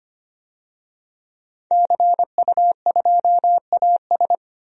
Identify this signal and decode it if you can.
Morse code, heard as a keyed tone, about 25 wpm, 700 Hz. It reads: CU2AS
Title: DXCC-100 Call Signs - 5 Characters - 25wpm